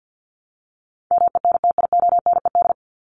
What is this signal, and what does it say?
Morse code, heard as a keyed tone, about 50 wpm, 700 Hz. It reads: MENTIONED
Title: ED Suffix - Top 100 Words with Base Words 50wpm